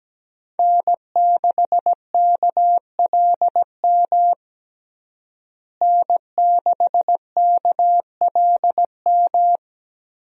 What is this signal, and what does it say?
Morse code, heard as a keyed tone, about 17 wpm, 700 Hz. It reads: N6KLM N6KLM
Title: USA Call Signs - 5 Characters - Repeated 2x - 17wpm